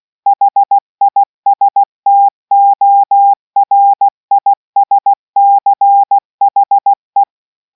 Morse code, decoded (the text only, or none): HISTORISCHE